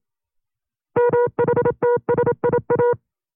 Morse code, heard as a keyed tone, about 27 wpm, 480 Hz. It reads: MHTSIA
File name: CW-practice/25WPM